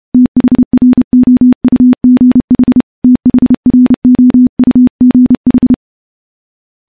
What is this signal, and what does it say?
Morse code, heard as a keyed tone, about 28 wpm, 250 Hz. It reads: THROUGH THROUGH
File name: morse / morse-training-30WPM